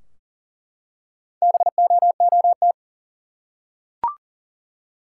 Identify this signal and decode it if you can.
Morse code, heard as a keyed tone, about 40 wpm, 700 Hz. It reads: BOOT
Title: Four Letter Words - 40wpm